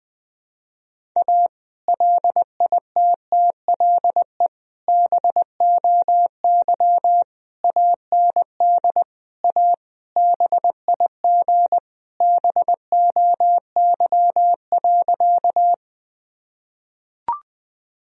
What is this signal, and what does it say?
Morse code, heard as a keyed tone, about 21 wpm, 700 Hz. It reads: A LITTLE BOY AND A BIG BOY.
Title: Sentences from Top 1000 Words - Review Part 1 - 20wpm